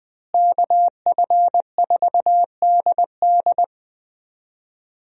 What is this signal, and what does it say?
Morse code, heard as a keyed tone, about 21 wpm, 700 Hz. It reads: KF4DD
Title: USA Call Signs - 5 Characters - Repeated 3x - 20wpm